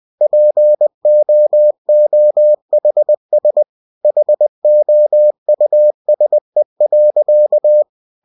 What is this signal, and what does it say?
Morse code, heard as a keyed tone, about 20 wpm, 600 Hz. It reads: POOHS HOUSE.
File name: cw-books / winnie-the-pooh / winnie-20wpm-page-0038